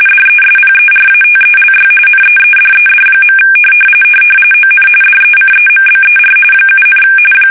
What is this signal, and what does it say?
RTTY, about 45.45 baud. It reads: DESPERATELY NEEDED. ON DECEMBER 7,
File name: fsk850 / battleshipnj